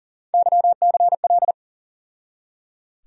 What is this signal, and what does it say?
Morse code, heard as a keyed tone, about 38 wpm, 700 Hz. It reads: YCL
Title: USA Call Sign Postfixes - 3 Letters 40wpm